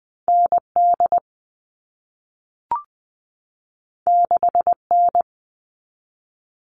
Morse code, heard as a keyed tone, about 20 wpm, 700 Hz. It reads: N D 6 N
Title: All Cut Numbers - 2-Digit 20wpm